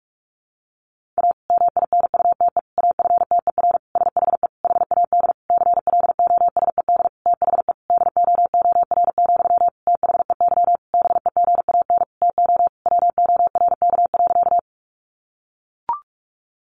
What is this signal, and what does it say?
Morse code, heard as a keyed tone, about 44 wpm, 700 Hz. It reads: A MINUTE AFTER SHE HAD CLOSED THE DOOR, THEY BEGAN TO WORK.
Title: Sentences from Top 1400 Words 45wpm